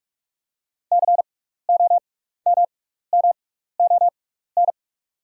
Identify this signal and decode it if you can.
Morse code, heard as a keyed tone, about 46 wpm, spaced as for 16 wpm, 690 Hz. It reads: COMMON
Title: 6x Character Spacing - Top 500 Words - 45wpm